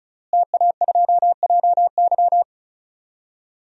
Morse code, heard as a keyed tone, about 34 wpm, 700 Hz. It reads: TA2JY
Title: DXCC-100 Call Signs - 5 Characters - Repeated 2x - 35wpm